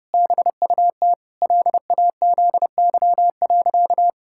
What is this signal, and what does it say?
Morse code, heard as a keyed tone, about 30 wpm, 700 Hz. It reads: BUT LAZY.
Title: Pangram Challenge v2 - 30wpm